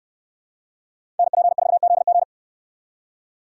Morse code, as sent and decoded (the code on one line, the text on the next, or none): -. .--. ....- --.. --.
NP4ZG